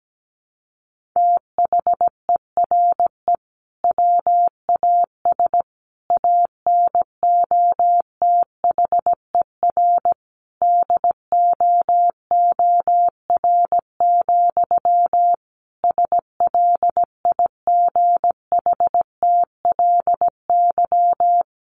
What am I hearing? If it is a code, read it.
Morse code, heard as a keyed tone, about 17 wpm, 700 Hz. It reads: THERE WAS ANOTHER DOOR, SLIGHTLY
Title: Sentences from Top 1300 Words 17wpm